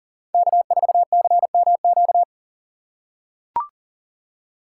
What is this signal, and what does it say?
Morse code, heard as a keyed tone, about 40 wpm, 700 Hz. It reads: K4CMQ
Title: USA Call Signs - 5 Characters - 4x Character Spacing - 40wpm